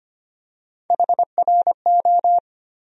Morse code, heard as a keyed tone, about 25 wpm, 700 Hz. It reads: HRO
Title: USA Call Sign Postfixes - 3 Letters 25wpm